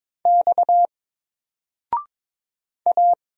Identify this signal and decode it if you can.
Morse code, heard as a keyed tone, about 22 wpm, 700 Hz. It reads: X A
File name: Single-Letter-Number-22wpm